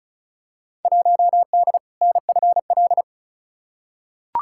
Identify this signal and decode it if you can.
Morse code, heard as a keyed tone, about 36 wpm, 700 Hz. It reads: 1D NFL
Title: Field Day Exchanges - 35wpm